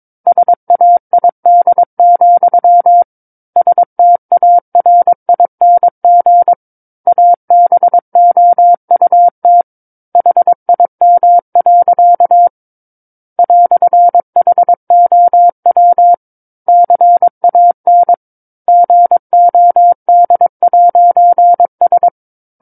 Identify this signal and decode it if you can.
Morse code, heard as a keyed tone, about 22 wpm, 700 Hz. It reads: SAID, STARING ABOUT HIM. "HOW CAN GOD'S
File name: Book One / 13-Chapter 13 - 22 WPM